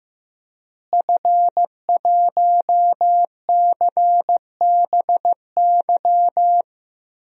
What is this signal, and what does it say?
Morse code, heard as a keyed tone, about 15 wpm, 700 Hz. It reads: F1CBY
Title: DXCC-100 Call Signs - 5 Characters - Repeated 3x - 15wpm